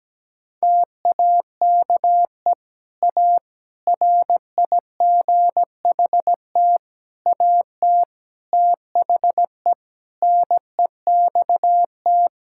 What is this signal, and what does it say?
Morse code, heard as a keyed tone, about 17 wpm, 700 Hz. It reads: TAKE A RIGHT AT THE NEXT
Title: Sentences from Top 1600 Words 17wpm